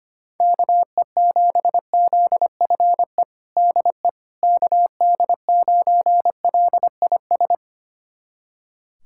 Morse code, heard as a keyed tone, about 25 wpm, 700 Hz. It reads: KE7ZFE DE KD9LIS
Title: USA Callsign DE Callsign 25wpm